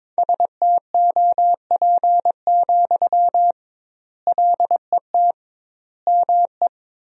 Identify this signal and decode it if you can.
Morse code, heard as a keyed tone, about 22 wpm, 700 Hz. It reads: STOP, LET ME
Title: 2x Word-Spacing - Sentences from Top 300 Words - 22wpm